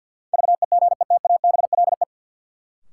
Morse code, heard as a keyed tone, about 51 wpm, 700 Hz. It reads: VEGETABLE
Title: Sentences from Top 1300 Words - New Word Drill - 50wpm